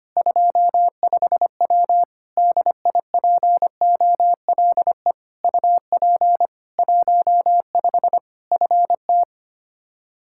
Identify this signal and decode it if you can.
Morse code, heard as a keyed tone, about 25 wpm, 700 Hz. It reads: 25W DIPOLE UP 15 FT
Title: Practice QSO - Exchange 2 - Info - 25wpm